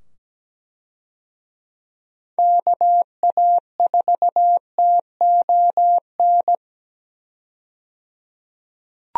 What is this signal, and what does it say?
Morse code, heard as a keyed tone, about 17 wpm, 700 Hz. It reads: KA4TON
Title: USA Call Signs - 6 Characters - 17wpm